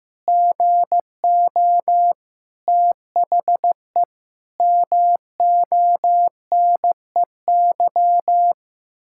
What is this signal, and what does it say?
Morse code, heard as a keyed tone, about 15 wpm, 700 Hz. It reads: GO THE MONEY
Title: Pangram Challenge v2 - 15wpm